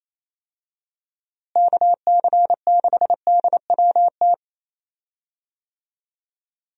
Morse code, heard as a keyed tone, about 28 wpm, 700 Hz. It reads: KC6DWT
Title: USA Call Signs - 6 Characters - Repeated 2x - 28wpm